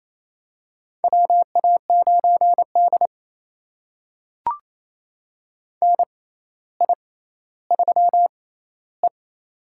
Morse code, heard as a keyed tone, about 28 wpm, spaced as for 14 wpm, 700 Hz. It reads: WA9D NI3E
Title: USA Call Signs - 6x Character Spacing - 28wpm